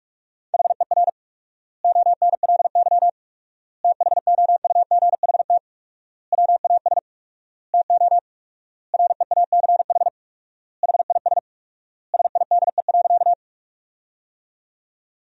Morse code, heard as a keyed tone, about 41 wpm, 700 Hz. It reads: HER ONLY THOUGHT WAS TO REACH HIS SIDE.
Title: 4x Word-Spacing - Sentences from Top 400 Words - 45wpm